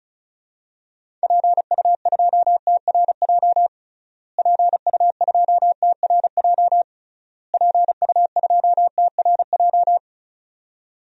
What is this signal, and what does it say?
Morse code, heard as a keyed tone, about 34 wpm, 700 Hz. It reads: PU2TRJ PU2TRJ PU2TRJ
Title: DXCC-100 Call Signs - 6 Characters - Repeated 3x - 35wpm